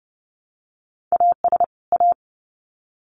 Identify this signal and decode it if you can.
Morse code, heard as a keyed tone, about 30 wpm, 700 Hz. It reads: AS A
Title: Sets of 2 Words 30wpm